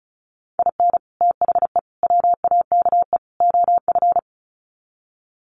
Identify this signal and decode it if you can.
Morse code, heard as a keyed tone, about 34 wpm, 700 Hz. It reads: IN THE WAKE OF
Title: Sets of 4 Words 35wpm